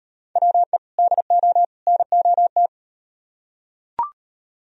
Morse code, heard as a keyed tone, about 38 wpm, 700 Hz. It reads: WE DO NOT